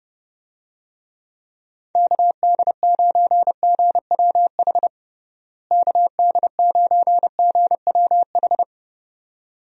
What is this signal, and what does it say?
Morse code, heard as a keyed tone, about 30 wpm, 700 Hz. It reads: KD9GWH KD9GWH
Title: USA Call Signs - 6 Characters - Repeated 2x - 30wpm